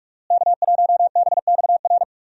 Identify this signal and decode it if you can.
Morse code, heard as a keyed tone, about 46 wpm, 700 Hz. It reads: K1BXR
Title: USA Call Signs - 5 Characters - Repeated 3x - 45wpm